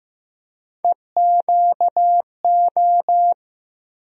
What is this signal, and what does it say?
Morse code, heard as a keyed tone, about 15 wpm, 700 Hz. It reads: EQO
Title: USA Call Sign Postfixes - 3 Letters 15wpm